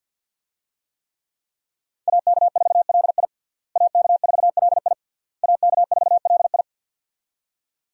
Morse code, heard as a keyed tone, about 46 wpm, 700 Hz. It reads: AK4LI AK4LI AK4LI
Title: USA Call Signs - 5 Characters - Repeated 3x - 50wpm